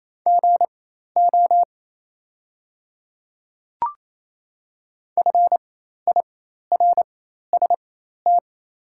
Morse code, heard as a keyed tone, about 28 wpm, spaced as for 13 wpm, 690 Hz. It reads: GO FIRST
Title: 4x Character Spacing - Top 100 Words - 28wpm